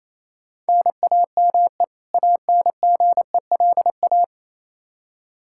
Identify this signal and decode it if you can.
Morse code, heard as a keyed tone, about 28 wpm, 700 Hz. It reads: NAME ANGELA
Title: K1USN SST QSO Practice 28wpm